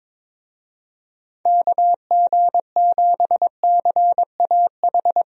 Morse code, heard as a keyed tone, about 22 wpm, 700 Hz. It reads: KG7CAH
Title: USA Call Signs - 6 Characters - Repeated 2x - 22wpm